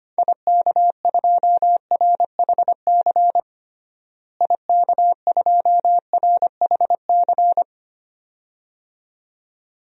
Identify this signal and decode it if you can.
Morse code, heard as a keyed tone, about 25 wpm, 700 Hz. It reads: IK2RHC IK2RHC
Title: DXCC-100 Call Signs - 6 Characters - Repeated 2x - 25wpm